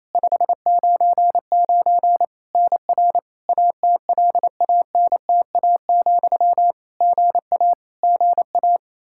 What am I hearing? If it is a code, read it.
Morse code, heard as a keyed tone, about 27 wpm, 700 Hz. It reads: 599 NR ATLANTA, GA GA
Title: Practice QSO - Exchange 1 - Beginning - 28wpm